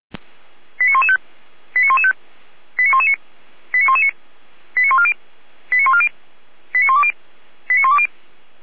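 Five-tone selective call (ZVEI), heard as quick runs of five tones, tones of about 70 ms. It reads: 89115 89115 89119 89119 89140 89140 89120 89120
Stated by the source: ZVEI five-tone selective call 89067 received on SDR, from the feuerwehr-orlamuende.org fire brigade audio server.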